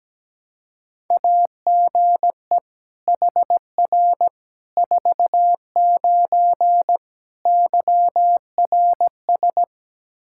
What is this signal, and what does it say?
Morse code, heard as a keyed tone, about 17 wpm, 700 Hz. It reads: AGE HR 49 YRS
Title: Practice QSO - Exchange 3 - More Info - 17wpm